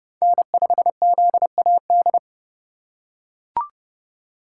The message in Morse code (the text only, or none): N5ZAD